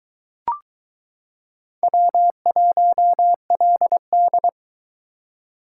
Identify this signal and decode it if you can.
Morse code, heard as a keyed tone, about 22 wpm, 700 Hz. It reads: W1LD